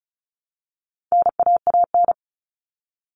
Morse code, heard as a keyed tone, about 34 wpm, 700 Hz. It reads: NAAN
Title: All Cut Numbers - 4-Digit 35wpm